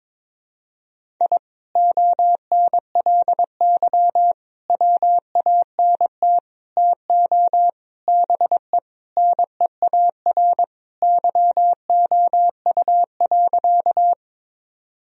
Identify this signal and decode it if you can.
Morse code, heard as a keyed tone, about 22 wpm, 700 Hz. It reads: I ONLY WANT TO BE NEAR YOU.
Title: Sentences from Top 1000 Words - Review Part 1 - 22wpm